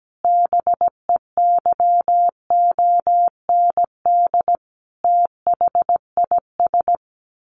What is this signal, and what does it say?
Morse code, heard as a keyed tone, about 17 wpm, 700 Hz. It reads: BEYOND THIS
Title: A Sound Of Thunder 17wpm